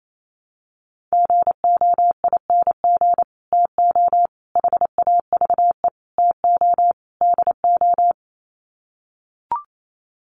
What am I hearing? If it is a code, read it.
Morse code, heard as a keyed tone, about 28 wpm, 700 Hz. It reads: GOING TO HAVE TO DO
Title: Sets of 5 Words Encore 28wpm